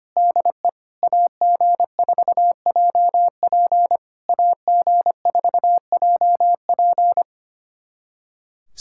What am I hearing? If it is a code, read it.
Morse code, heard as a keyed tone, about 25 wpm, 690 Hz. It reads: DE AG4JP AG4JP
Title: SOTA QSO Practice 25wpm